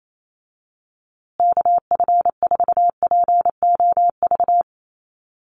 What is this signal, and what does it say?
Morse code, heard as a keyed tone, about 28 wpm, 700 Hz. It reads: KF4POV
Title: USA Call Signs Repeated 3x 28wpm